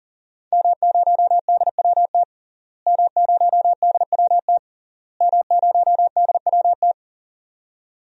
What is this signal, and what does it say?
Morse code, heard as a keyed tone, about 40 wpm, 700 Hz. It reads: M0DWT M0DWT M0DWT
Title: DXCC-100 Call Signs - 5 Characters - Repeated 3x - 40wpm